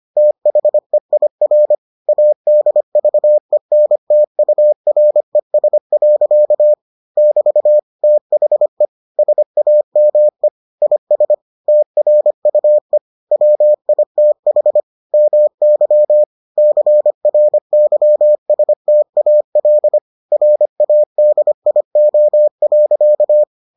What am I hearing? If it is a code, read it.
Morse code, heard as a keyed tone, about 25 wpm, 600 Hz. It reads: THEIR ADVENTURES. <BT> THE SAME IS TRUE WITH MY CRYSTAL RADIO.